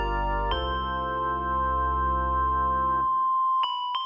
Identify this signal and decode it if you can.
There is no code.